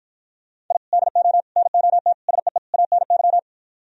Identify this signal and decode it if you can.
Morse code, heard as a keyed tone, about 52 wpm, 700 Hz. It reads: I DO NOT SEE ANY